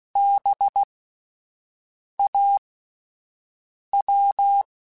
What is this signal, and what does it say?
Morse code, heard as a keyed tone, about 16 wpm, spaced as for 5 wpm, 800 Hz. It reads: BAW
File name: Alphabetical / Alphabetical mp3s / Bs @5